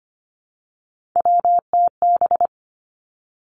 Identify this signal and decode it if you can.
Morse code, heard as a keyed tone, about 25 wpm, 700 Hz. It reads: WTB
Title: Three Challenging Letter-Number Combinations 25wpm